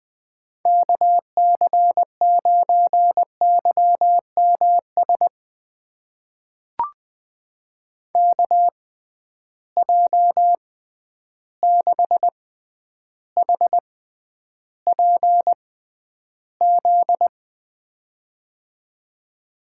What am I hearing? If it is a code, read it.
Morse code, heard as a keyed tone, about 20 wpm, spaced as for 10 wpm, 700 Hz. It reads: KC9YMS KJ6HPZ
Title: USA Call Signs - 6 Characters - 6x Character Spacing - 20wpm